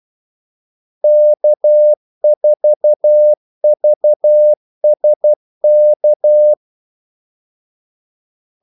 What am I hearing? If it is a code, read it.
Morse code, heard as a keyed tone, about 12 wpm, 600 Hz. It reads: K4VSK
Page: 2 ekstra ordmellemrum mellem kaldesignalerne